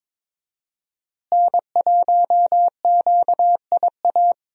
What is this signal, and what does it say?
Morse code, heard as a keyed tone, about 22 wpm, 700 Hz. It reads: N1QIA
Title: USA Call Signs - 5 Characters - Repeated 3x - 22wpm